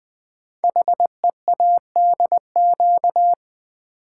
Morse code, heard as a keyed tone, about 20 wpm, 700 Hz. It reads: HEADQ
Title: Build A Word - 5 Letters At A Time - 12-Letter Words - 20wpm